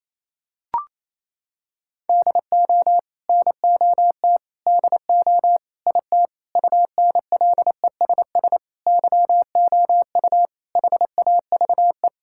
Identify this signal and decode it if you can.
Morse code, heard as a keyed tone, about 28 wpm, 700 Hz. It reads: DO NOT DO IT UNLESS YOU HAVE
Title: Sentences from Top 1000 Words - Review Part 6 - 28wpm